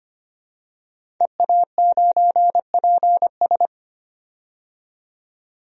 Morse code, heard as a keyed tone, about 25 wpm, 700 Hz. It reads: EA9PS
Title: DXCC-100 Call Signs - 5 Characters - 25wpm